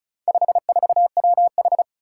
Morse code, heard as a keyed tone, about 36 wpm, 700 Hz. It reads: 54WH
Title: Four Challenging Letter-Number Combinations 35wpm